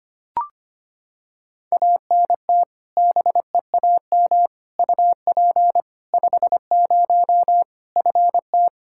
Morse code, heard as a keyed tone, about 25 wpm, 700 Hz. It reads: ANT BEAM UP 50 FT
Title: Practice QSO - Exchange 2 - Info - 25wpm